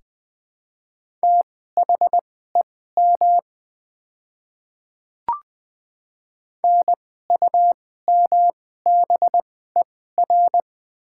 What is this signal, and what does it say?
Morse code, heard as a keyed tone, about 21 wpm, spaced as for 15 wpm, 700 Hz. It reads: THEM NUMBER
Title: 2x Character Spacing - Top 100 Words - 20wpm